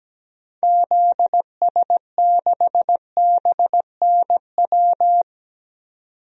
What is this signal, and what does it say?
Morse code, heard as a keyed tone, about 17 wpm, 700 Hz. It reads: ZS6BNW
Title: DXCC-100 Call Signs - 6 Characters - 17wpm